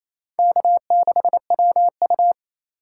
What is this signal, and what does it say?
Morse code, heard as a keyed tone, about 34 wpm, 700 Hz. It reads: K6WU
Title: Speed Racing - USA 1x2 Call Signs 28wpm